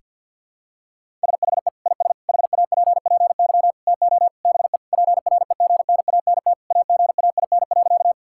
Morse code, heard as a keyed tone, about 52 wpm, 700 Hz. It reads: SHE IS HAPPY TO BE PREGNANT AGAIN.
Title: Sentences from Top 1000 Words - Review Part 6 - 50wpm